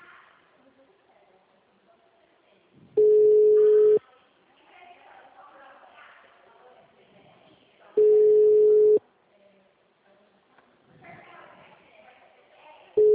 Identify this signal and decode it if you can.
Ringback tone (Europe, 425 Hz), with speech.